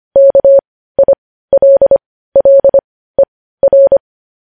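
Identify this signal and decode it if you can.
Morse code, heard as a keyed tone, about 25 wpm, spaced as for 15 wpm, 550 Hz. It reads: KILLER
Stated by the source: Morse code news headlines at 15 WPM.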